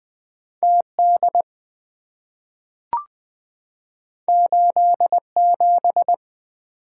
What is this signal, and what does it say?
Morse code, heard as a keyed tone, about 20 wpm, 700 Hz. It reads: TD 87
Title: Two Letter-Number Combinations - 20wpm